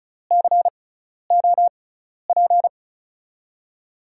Morse code, heard as a keyed tone, about 36 wpm, spaced as for 12 wpm, 700 Hz. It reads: COP
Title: USA Call Signs - 6x Character Spacing - 35wpm